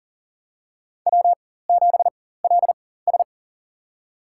Morse code, heard as a keyed tone, about 40 wpm, spaced as for 19 wpm, 700 Hz. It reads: W7LS
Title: USA Call Signs - 4x Character Spacing - 40wpm